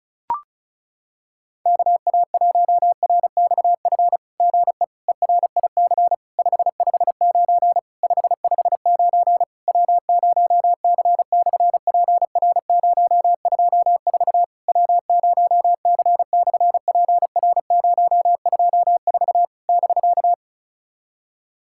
Morse code, heard as a keyed tone, about 36 wpm, 700 Hz. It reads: KA1RXF GE ERIC 559 559 W0C/PR024 W0C/PR024 <BK>
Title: SOTA QSO Practice 35wpm